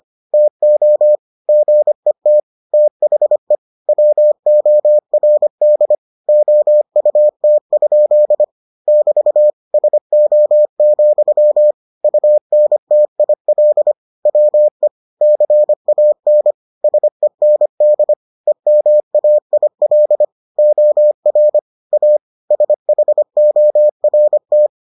Morse code, heard as a keyed tone, about 25 wpm, 600 Hz. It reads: TO GET THE WORD OUT? <BT> SO, UNTIL WE CAN SEND EMAIL OR A SHORT